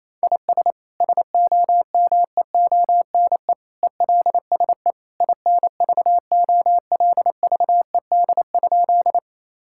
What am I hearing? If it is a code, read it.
Morse code, heard as a keyed tone, about 28 wpm, 700 Hz. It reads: IS SOMEONE ELSE INVOLVED?
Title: Sentences from Top 1000 Words - Review Part 5 - 28wpm